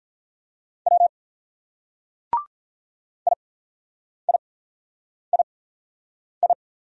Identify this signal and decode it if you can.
Morse code, heard as a keyed tone, about 44 wpm, 700 Hz. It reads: W I I I I
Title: Speed-Racing Single Letters 35wpm